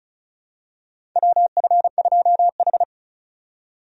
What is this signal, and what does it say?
Morse code, heard as a keyed tone, about 36 wpm, 700 Hz. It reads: WF2H
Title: USA Call Signs - 4 Characters - 35wpm